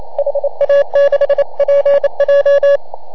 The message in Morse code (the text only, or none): HA6PJ